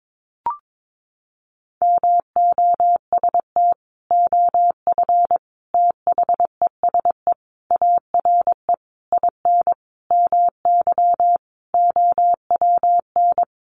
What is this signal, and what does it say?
Morse code, heard as a keyed tone, about 22 wpm, 700 Hz. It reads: MOST OF THESE ARE IN MY OWN
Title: Sentences from Top 1200 Words 22wpm